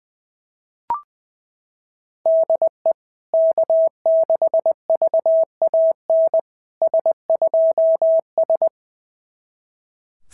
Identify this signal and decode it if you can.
Morse code, heard as a keyed tone, about 20 wpm, 650 Hz. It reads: DE K6VAN S2S
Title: SOTA QSO Practice 20wpm